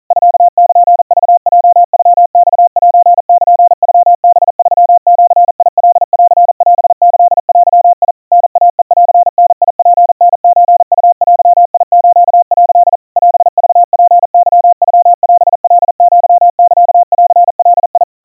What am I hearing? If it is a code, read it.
Morse code (Japanese Wabun code), heard as a keyed tone, about 40 wpm, 700 Hz. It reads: ミルクヲノマセルノハ3ジカンオキデ タイヘンダツタソウデス」 オクツケノオカアサンガ
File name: kawa / Morse sample / kamyu 150 cpm 700 Hz Crispy